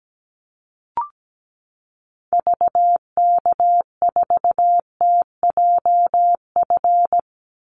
Morse code, heard as a keyed tone, about 17 wpm, 700 Hz. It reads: VK4TJF
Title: Call Signs - 17wpm